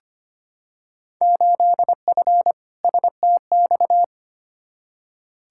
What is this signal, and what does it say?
Morse code, heard as a keyed tone, about 25 wpm, 700 Hz. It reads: 8F STX
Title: Field Day Exchanges Encore - 25wpm